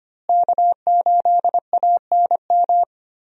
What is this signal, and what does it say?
Morse code, heard as a keyed tone, about 25 wpm, 700 Hz. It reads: K8ANM
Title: USA Call Signs - 5 Characters - Repeated 3x - 25wpm